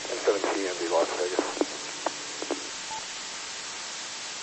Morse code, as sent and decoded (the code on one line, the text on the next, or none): none